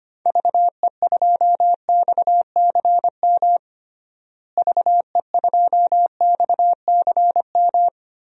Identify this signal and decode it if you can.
Morse code, heard as a keyed tone, about 25 wpm, 700 Hz. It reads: VE2XCM VE2XCM
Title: DXCC-100 Call Signs - 6 Characters - Repeated 2x - 25wpm